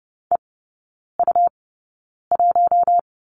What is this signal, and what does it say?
Morse code, heard as a keyed tone, about 30 wpm, spaced as for 9 wpm, 700 Hz. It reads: EU1
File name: Lesson 18e - Letter m - Three Characters 30wpm